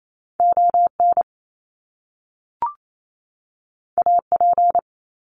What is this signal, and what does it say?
Morse code, heard as a keyed tone, about 28 wpm, 700 Hz. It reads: ON AP
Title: Top Two Letter Combinations in Words 28wpm